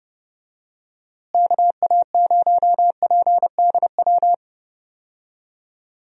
Morse code, heard as a keyed tone, about 30 wpm, 700 Hz. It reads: KA0PDW
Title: USA Call Signs - 6 Characters - Repeated 3x - 30wpm